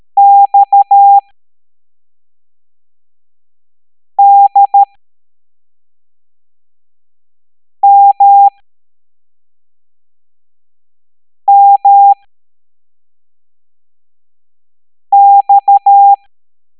Morse code, heard as a keyed tone, about 13 wpm, spaced as for 3 wpm, 800 Hz. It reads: XDMMX
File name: hamdata / L02-EX01-MDX